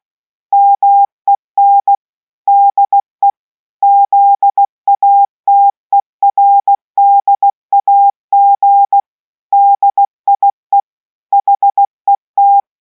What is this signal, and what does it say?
Morse code, heard as a keyed tone, about 16 wpm, 800 Hz. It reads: MEN DE ZATERDAG DIE HET